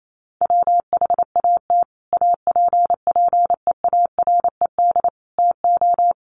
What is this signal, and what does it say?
Morse code, heard as a keyed tone, about 28 wpm, 700 Hz. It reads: WHAT APPEARED TO
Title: Sets of 5 Words Encore2 28wpm